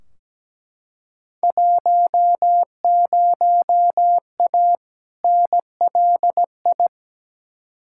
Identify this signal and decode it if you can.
Morse code, heard as a keyed tone, about 17 wpm, 700 Hz. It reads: 10A NLI
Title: Field Day Exchanges Encore - 17wpm